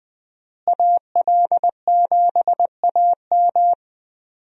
Morse code, heard as a keyed tone, about 20 wpm, 700 Hz. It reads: AL7AM
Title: USA Call Signs - 5 Characters - Repeated 2x - 20wpm